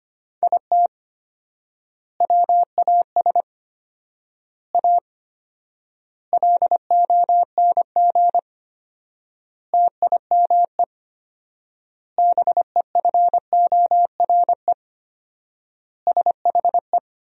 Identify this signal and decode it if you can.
Morse code, heard as a keyed tone, about 25 wpm, 700 Hz. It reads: IT WAS A LONG TIME BEFORE SHE
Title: 4x Word-Spacing - Sentences from Top 500 Words - 25wpm